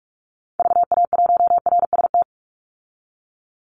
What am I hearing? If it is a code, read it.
Morse code, heard as a keyed tone, about 44 wpm, 700 Hz. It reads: VA1RST
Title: Call Signs Repeated 3x - 45wpm